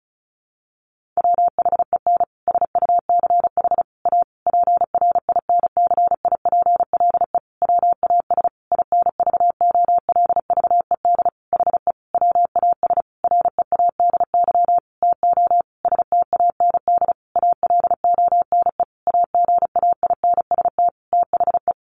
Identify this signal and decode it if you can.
Morse code, heard as a keyed tone, about 34 wpm, 700 Hz. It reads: WHEN SUCH A PRINCIPLE WAS INVOLVED HE WAS READY TO STAND ALONE AGAINST THE
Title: Sentences from Top 1400 Words 35wpm